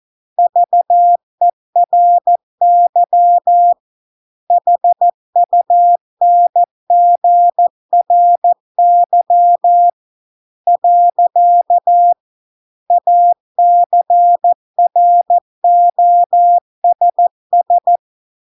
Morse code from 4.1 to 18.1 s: HUNGRY . ACROSS